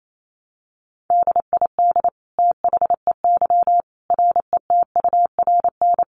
Morse code, heard as a keyed tone, about 28 wpm, 700 Hz. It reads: DID THEY RETURN
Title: Sentences from Top 800 Words 28wpm